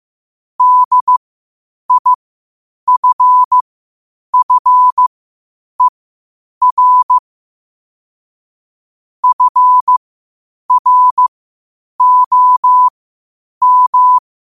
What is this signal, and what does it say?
Morse code, heard as a keyed tone, about 15 wpm, spaced as for 8 wpm, 1000 Hz. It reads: DIFFER FROM
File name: qotdfiles / 8wpm